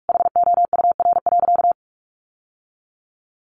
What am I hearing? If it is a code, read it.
Morse code, heard as a keyed tone, about 44 wpm, 700 Hz. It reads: HOUR.
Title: Sentences from Top 1300 Words 45wpm